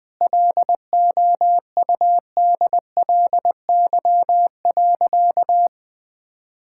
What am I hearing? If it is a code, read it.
Morse code, heard as a keyed tone, about 20 wpm, 700 Hz. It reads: LOUDLY.
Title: Sentences with 2-Word Phrases - 20wpm